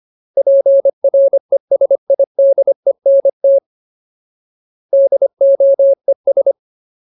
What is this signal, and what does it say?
Morse code, heard as a keyed tone, about 25 wpm, 550 Hz. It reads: PRESIDENT DOES
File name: Ultimate Word List / 25WPM Ultimate